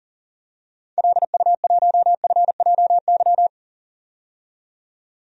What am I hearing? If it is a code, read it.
Morse code, heard as a keyed tone, about 39 wpm, 700 Hz. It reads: LU1FJY
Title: DXCC-100 Call Signs - 6 Characters - 40wpm